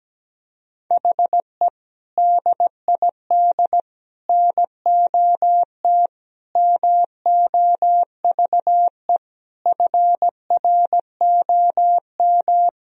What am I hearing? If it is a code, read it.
Morse code, heard as a keyed tone, about 17 wpm, 700 Hz. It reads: HE DID NOT MOVE FROM
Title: Sentences from Top 1000 Words - Review Part 1 - 17wpm